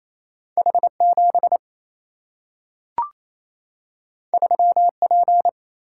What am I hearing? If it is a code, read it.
Morse code, heard as a keyed tone, about 28 wpm, 700 Hz. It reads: H7 3P
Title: Single-Letter Single-Number 28wpm